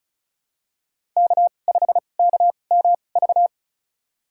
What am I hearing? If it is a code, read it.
Morse code, heard as a keyed tone, about 34 wpm, spaced as for 25 wpm, 700 Hz. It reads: K5KMV
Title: USA Call Signs - 5 Characters - 2x Character Spacing - 35wpm